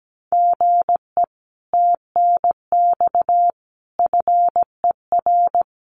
Morse code, heard as a keyed tone, about 17 wpm, 700 Hz. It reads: GE TNX FER
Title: Common QSO Phrases Encore 17wpm